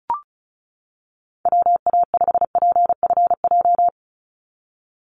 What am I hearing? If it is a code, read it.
Morse code, heard as a keyed tone, about 34 wpm, 700 Hz. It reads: WA5PFJ
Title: Call Signs - 35wpm